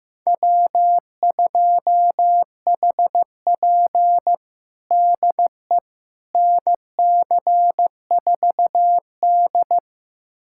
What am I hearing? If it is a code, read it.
Morse code, heard as a keyed tone, about 15 wpm, 700 Hz. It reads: W2HP DE NC4D
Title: USA Callsign DE Callsign - 4 Characters - 15wpm